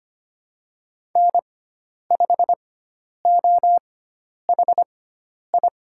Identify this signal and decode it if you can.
Morse code, heard as a keyed tone, about 25 wpm, spaced as for 10 wpm, 700 Hz. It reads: N5OHI
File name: Lesson 16g - Letter c - Call Signs 25wpm